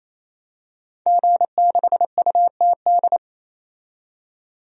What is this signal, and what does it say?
Morse code, heard as a keyed tone, about 28 wpm, 700 Hz. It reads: G6UTD
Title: DXCC-100 Call Signs - 5 Characters - 28wpm